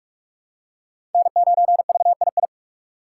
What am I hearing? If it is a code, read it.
Morse code, heard as a keyed tone, about 43 wpm, 700 Hz. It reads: N9VII
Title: USA Call Signs - 4x Character Spacing - 45wpm